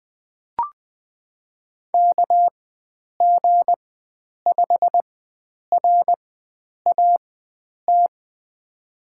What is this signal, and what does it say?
Morse code, heard as a keyed tone, about 20 wpm, spaced as for 9 wpm, 700 Hz. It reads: KG5RAT
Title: USA Call Signs - 6 Characters - 4x Character Spacing - 20wpm